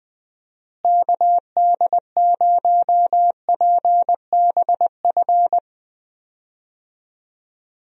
Morse code, heard as a keyed tone, about 20 wpm, 700 Hz. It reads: KD0PBF
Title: USA Call Signs - 6 Characters - Repeated 3x - 20wpm